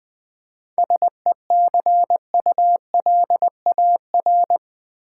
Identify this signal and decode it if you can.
Morse code, heard as a keyed tone, about 20 wpm, 700 Hz. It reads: SECULAR
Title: 7 Letter Words - First 3 Letters Given - 20wpm